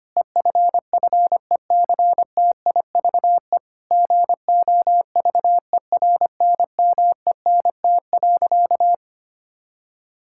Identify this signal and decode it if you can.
Morse code, heard as a keyed tone, about 25 wpm, 700 Hz. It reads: EFFECTIVE GOVERNMENT.
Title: Sentences from Top 1000 Words - Review Part 6 - 25wpm